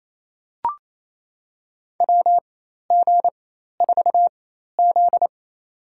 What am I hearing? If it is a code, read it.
Morse code, heard as a keyed tone, about 28 wpm, spaced as for 13 wpm, 700 Hz. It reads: WG4Z
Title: USA Call Signs - 4 Characters - 4x Character Spacing - 28wpm